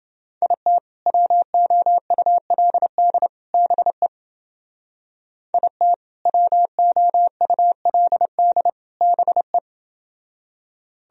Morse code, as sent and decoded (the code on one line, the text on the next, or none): .. - / .-- --- ..- .-.. -.. / -... / .. - / .-- --- ..- .-.. -.. / -... .
IT WOULD BE IT WOULD BE